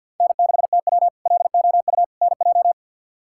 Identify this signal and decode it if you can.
Morse code, heard as a keyed tone, about 51 wpm, 700 Hz. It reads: N6TW LOU NJ
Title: K1USN SST QSO Practice - 50wpm